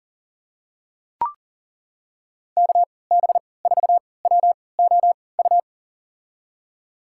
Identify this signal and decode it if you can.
Morse code, heard as a keyed tone, about 40 wpm, spaced as for 23 wpm, 700 Hz. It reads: KB4WOU
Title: USA Call Signs - 6 Characters - 3x Character Spacing - 40wpm